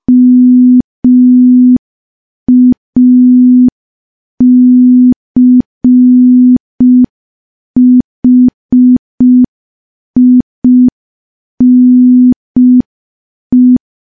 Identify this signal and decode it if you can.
Morse code, heard as a keyed tone, about 5 wpm, 250 Hz. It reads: MACHINE
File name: morse / morse-training-05WPM